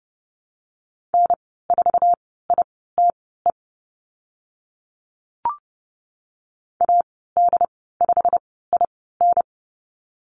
Morse code, heard as a keyed tone, about 30 wpm, spaced as for 17 wpm, 700 Hz. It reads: N4ITE AD5IN
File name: Lesson 15g - Number 5 - Call Signs 30wpm